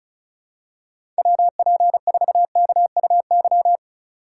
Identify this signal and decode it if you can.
Morse code, heard as a keyed tone, about 36 wpm, 700 Hz. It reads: WP4KUY
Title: USA Call Signs - 6 Characters - 3x Character Spacing - 35wpm